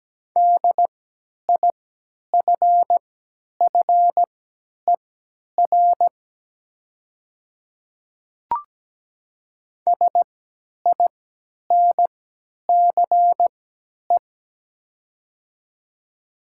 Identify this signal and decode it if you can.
Morse code, heard as a keyed tone, about 17 wpm, spaced as for 10 wpm, 700 Hz. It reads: DIFFER SINCE
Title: 3x Character Spacing - Top 500 Words - 17wpm